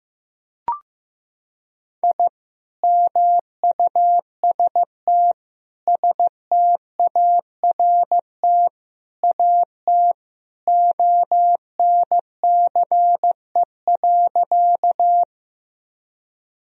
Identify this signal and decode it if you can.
Morse code, heard as a keyed tone, about 15 wpm, 700 Hz. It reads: I MUST START AT ONCE.
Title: Sentences from Top 1000 Words - Review Part 1 - 15wpm